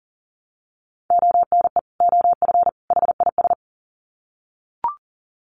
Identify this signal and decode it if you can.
Morse code, heard as a keyed tone, about 40 wpm, 700 Hz. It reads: ONE OF HIS
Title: Sets of 3 Words 40wpm